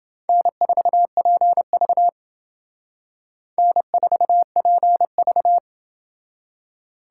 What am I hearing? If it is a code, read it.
Morse code, heard as a keyed tone, about 28 wpm, 700 Hz. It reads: N4PV N4PV